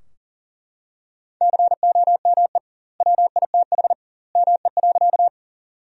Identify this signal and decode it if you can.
Morse code, heard as a keyed tone, about 40 wpm, 700 Hz. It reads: COME WITH ME.
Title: 2x Word-Spacing - Sentences from Top 200 Words - 40wpm